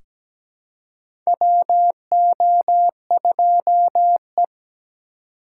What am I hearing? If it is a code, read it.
Morse code, heard as a keyed tone, about 17 wpm, 700 Hz. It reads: WO2E
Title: K1USN SST QSO Practice - 17wpm